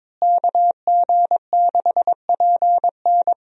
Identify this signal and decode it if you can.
Morse code, heard as a keyed tone, about 22 wpm, 700 Hz. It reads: KG6PN
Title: USA Call Signs - 5 Characters - Repeated 3x - 22wpm